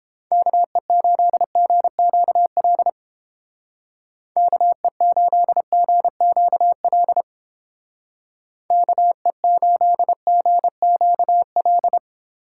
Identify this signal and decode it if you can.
Morse code, heard as a keyed tone, about 31 wpm, 700 Hz. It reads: KE8GQL KE8GQL KE8GQL